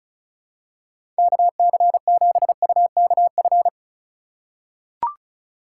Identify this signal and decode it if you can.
Morse code, heard as a keyed tone, about 34 wpm, 700 Hz. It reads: KC7UKF
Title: USA Call Signs - 4x Character Spacing - 35wpm